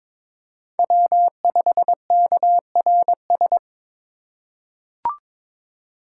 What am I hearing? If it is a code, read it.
Morse code, heard as a keyed tone, about 22 wpm, 700 Hz. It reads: W5KRS
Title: USA Call Signs - 2x Character Spacing - 22wpm